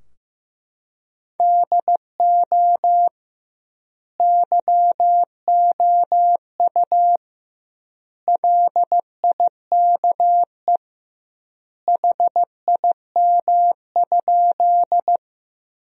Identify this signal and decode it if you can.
Morse code, heard as a keyed tone, about 15 wpm, 700 Hz. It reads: DO YOU LIKE HIM?
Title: 2x Word-Spacing - Sentences from Top 100 Words - 15wpm